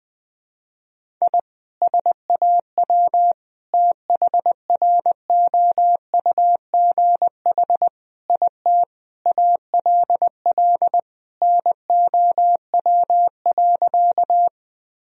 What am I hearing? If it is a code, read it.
Morse code, heard as a keyed tone, about 21 wpm, 700 Hz. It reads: I SAW THROUGH IT ALL NOW.
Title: Sentences from Top 1000 Words - Review Part 1 - 20wpm